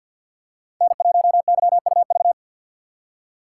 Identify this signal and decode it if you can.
Morse code, heard as a keyed tone, about 52 wpm, 700 Hz. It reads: N1YUV
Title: USA Call Signs - 5 Characters - 3x Character Spacing - 50wpm